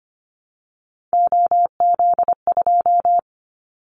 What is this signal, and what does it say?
Morse code, heard as a keyed tone, about 25 wpm, 700 Hz. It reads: OZ2
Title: Callsign Prefixes for Top 50 DXCC - 25wpm